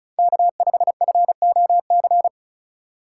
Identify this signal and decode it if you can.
Morse code, heard as a keyed tone, about 36 wpm, 700 Hz. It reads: K5FOC
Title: USA Call Signs - 5 Characters - 2x Character Spacing - 35wpm